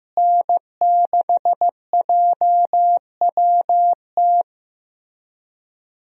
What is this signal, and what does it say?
Morse code, heard as a keyed tone, about 15 wpm, 700 Hz. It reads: N6JWT
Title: USA Call Signs - 5 Characters - Repeated 2x - 15wpm